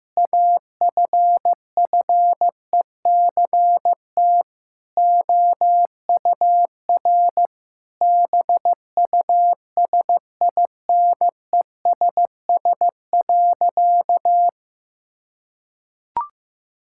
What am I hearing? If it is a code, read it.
Morse code, heard as a keyed tone, about 16 wpm, 700 Hz. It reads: AFFECT OUR BUSINESS.
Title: Sentences from Top 1000 Words - Review Part 6 - 15wpm